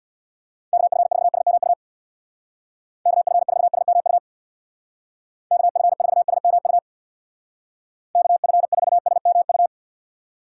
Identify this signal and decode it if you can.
Morse code, heard as a keyed tone, about 62 wpm, 700 Hz. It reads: KF4SMU KF4SMU KF4SMU KF4SMU
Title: Speed Racing - USA 2x3 Call Signs 50wpm